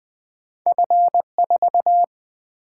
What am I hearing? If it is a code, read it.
Morse code, heard as a keyed tone, about 27 wpm, spaced as for 24 wpm, 700 Hz. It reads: F4
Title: Speed-Racing - Two Letter-Number Combinations - 20wpm